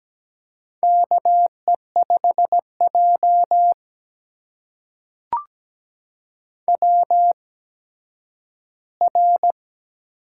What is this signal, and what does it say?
Morse code, heard as a keyed tone, about 17 wpm, spaced as for 10 wpm, 700 Hz. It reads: KE5J WR
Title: USA Call Signs - 4 Characters - 8x Character Spacing - 17wpm